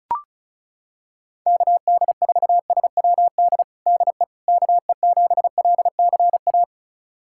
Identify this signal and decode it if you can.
Morse code, heard as a keyed tone, about 34 wpm, 700 Hz. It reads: KD4SWD DE KE7LCA
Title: USA Callsign DE Callsign - 6 Characters - 35wpm